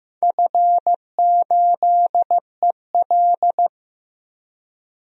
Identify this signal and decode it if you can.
Morse code, heard as a keyed tone, about 15 wpm, 700 Hz. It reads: F8EL
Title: DXCC-100 Call Signs - 4 Characters - Repeated 3x - 15wpm